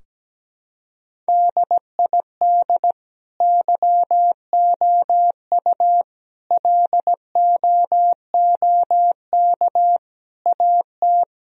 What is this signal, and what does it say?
Morse code, heard as a keyed tone, about 17 wpm, 700 Hz. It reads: DID YOU LOOK AT
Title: Sentences from Top 1000 Words - Review Part 4 - 17wpm